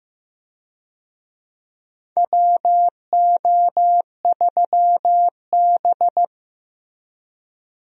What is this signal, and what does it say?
Morse code, heard as a keyed tone, about 15 wpm, 700 Hz. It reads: WO3B
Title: USA Call Signs - 4 Characters - Repeated 3x - 15wpm